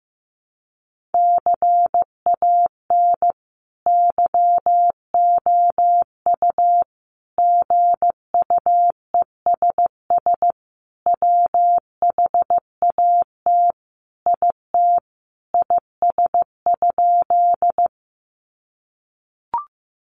Morse code, heard as a keyed tone, about 15 wpm, 700 Hz. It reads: CAN YOU GUESS WHAT IT IS?
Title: Sentences from Top 1500 Words 15wpm